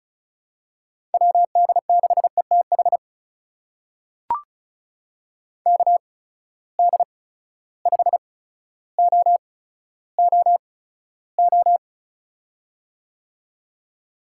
Morse code, heard as a keyed tone, about 34 wpm, spaced as for 15 wpm, 700 Hz. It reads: WD6ETH KD5OOO
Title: USA Call Signs - 6 Characters - 8x Character Spacing - 35wpm